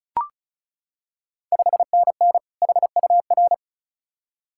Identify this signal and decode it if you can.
Morse code, heard as a keyed tone, about 34 wpm, 690 Hz. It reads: 5NN HUR
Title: Ontario QSO Party Exchange 35wpm